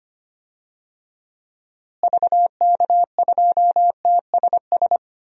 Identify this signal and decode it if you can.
Morse code, heard as a keyed tone, about 25 wpm, 700 Hz. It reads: VK2TSS
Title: DXCC-100 Call Signs - 6 Characters - Repeated 2x - 25wpm